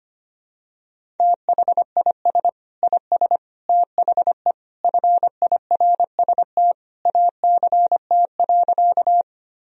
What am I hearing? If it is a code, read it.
Morse code, heard as a keyed tone, about 25 wpm, 700 Hz. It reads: THIS IS THE FIRST ACT.
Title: Sentences from Top 1000 Words - Review Part 1 - 25wpm